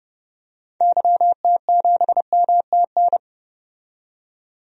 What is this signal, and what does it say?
Morse code, heard as a keyed tone, about 29 wpm, 700 Hz. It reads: YT7MTN
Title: DXCC-100 Call Signs - 6 Characters - 30wpm